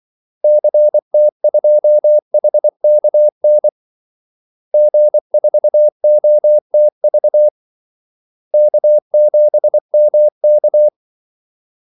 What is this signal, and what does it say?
Morse code, heard as a keyed tone, about 24 wpm, 600 Hz. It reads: CT2HKN G4OTV K7MK
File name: Callsigns-24wpm